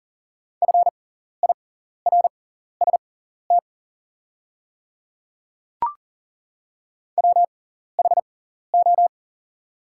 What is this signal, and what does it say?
Morse code, heard as a keyed tone, about 40 wpm, spaced as for 14 wpm, 700 Hz. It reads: FIRST WHO
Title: 6x Character Spacing - Top 100 Words - 40wpm